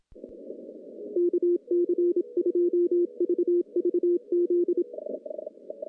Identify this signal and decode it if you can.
Morse code, heard as a keyed tone, about 27 wpm, 350 Hz. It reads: KC2VVZ
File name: contests / CW-SS-k5tr- -0000